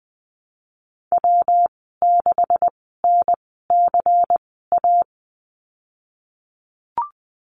Morse code, heard as a keyed tone, about 20 wpm, spaced as for 15 wpm, 700 Hz. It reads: W6NCA
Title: 213 Lesson 34g - Number 8 - Call Signs 20wpm